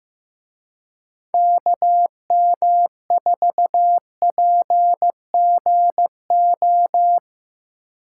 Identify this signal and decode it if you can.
Morse code, heard as a keyed tone, about 17 wpm, 700 Hz. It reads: KM4PGO
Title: Speed Racing - USA 2x3 Call Signs 15wpm